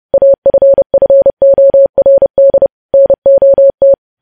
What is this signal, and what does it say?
Morse code, heard as a keyed tone, about 30 wpm, 550 Hz. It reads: AFFORD NOT
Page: news headlines in morse code at 30 wpm